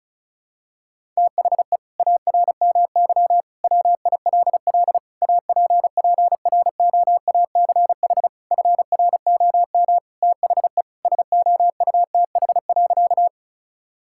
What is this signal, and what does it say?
Morse code, heard as a keyed tone, about 36 wpm, 690 Hz. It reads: THE ARMY WILL APPROACH FROM THE SOUTH.
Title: Sentences from Top 1000 Words - Review Part 6 - 35wpm